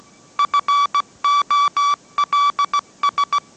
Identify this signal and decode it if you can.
Morse code, heard as a keyed tone, about 17 wpm, 1150 Hz. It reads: FOLS
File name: mrx / mrx080 18wpm